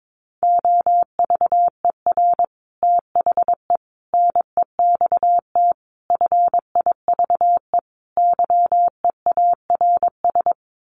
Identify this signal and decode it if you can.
Morse code, heard as a keyed tone, about 22 wpm, 700 Hz. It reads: OVER THE NEXT FIVE YEARS
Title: Sets of 5 Words 22wpm